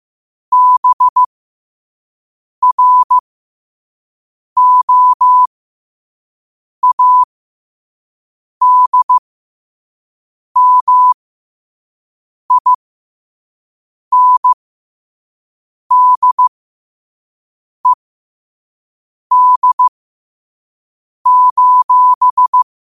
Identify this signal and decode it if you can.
Morse code, heard as a keyed tone, about 15 wpm, spaced as for 5 wpm, 1000 Hz. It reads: BROADMINDED:
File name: qotdfiles / 5wpm